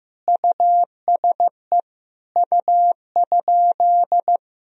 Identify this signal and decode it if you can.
Morse code, heard as a keyed tone, about 15 wpm, 700 Hz. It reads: USE U?
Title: Sentences from Top 200 Words and Abbreviations 15wpm